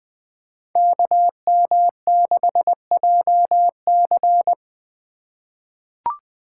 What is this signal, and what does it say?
Morse code, heard as a keyed tone, about 20 wpm, 700 Hz. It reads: KM6JC
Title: Code Groups - 5 Letter:Numbers - 20wpm